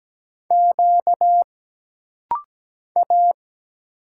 Morse code, heard as a keyed tone, about 17 wpm, 700 Hz. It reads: Q A
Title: Single Letters - Rapid-Fire 17wpm